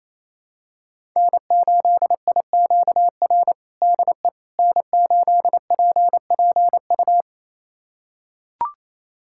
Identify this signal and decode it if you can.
Morse code, heard as a keyed tone, about 27 wpm, 700 Hz. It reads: N8IQR DE N8PPU
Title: USA Callsign DE Callsign - 5 Characters - 28wpm